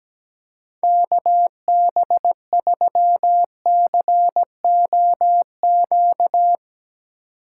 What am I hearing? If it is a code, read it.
Morse code, heard as a keyed tone, about 17 wpm, 700 Hz. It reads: KB3COQ
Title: USA Call Signs - 4x Character Spacing - 17wpm